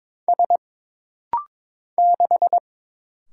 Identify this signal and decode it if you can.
Morse code, heard as a keyed tone, about 22 wpm, 700 Hz. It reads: S 6
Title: Single Letter-Number - Rapid-Fire 22wpm